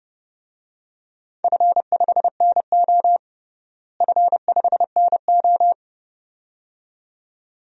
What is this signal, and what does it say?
Morse code, heard as a keyed tone, about 30 wpm, 700 Hz. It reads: F5NO F5NO
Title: DXCC-100 Call Signs - 4 Characters - Repeated 2x - 30wpm